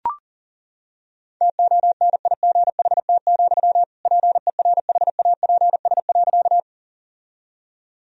Morse code, heard as a keyed tone, about 37 wpm, 700 Hz. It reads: TONIGHT, PERHAPS.
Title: Sentences from Top 1000 Words - Review Part 3 - 40wpm